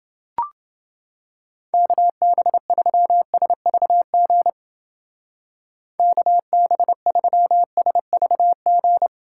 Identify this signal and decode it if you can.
Morse code, heard as a keyed tone, about 29 wpm, 700 Hz. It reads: KB3SVG KB3SVG